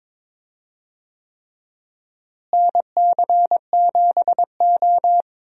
Morse code, heard as a keyed tone, about 21 wpm, 700 Hz. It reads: NC7O
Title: USA Call Signs - 4 Characters - Repeated 3x - 22wpm